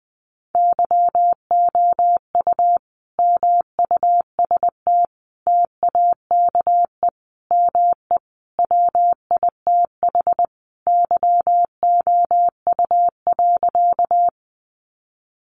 Morse code, heard as a keyed tone, about 20 wpm, 700 Hz. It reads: YOU MUST TAKE ME WITH YOU.
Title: Sentences from Top 200 Words 20wpm